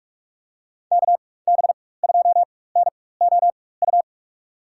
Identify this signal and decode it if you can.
Morse code, heard as a keyed tone, about 46 wpm, spaced as for 21 wpm, 700 Hz. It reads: KB2NOU
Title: USA Call Signs - 6 Characters - 4x Character Spacing - 45wpm